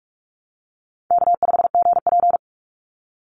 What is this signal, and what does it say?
Morse code, heard as a keyed tone, about 46 wpm, 700 Hz. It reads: K5GP
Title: Easy Call Signs - 45wpm